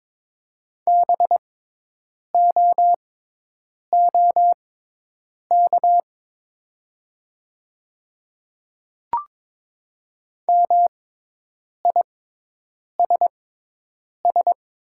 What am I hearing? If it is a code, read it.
Morse code, heard as a keyed tone, about 22 wpm, spaced as for 8 wpm, 700 Hz. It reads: BOOK MISS
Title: 6x Character Spacing - Top 500 Words - 22wpm